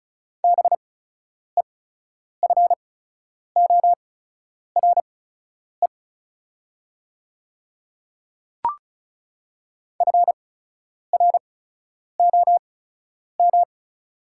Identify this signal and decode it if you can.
Morse code, heard as a keyed tone, about 35 wpm, spaced as for 10 wpm, 690 Hz. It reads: BEFORE FROM
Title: 8x Character Spacing - Top 200 Words - 35wpm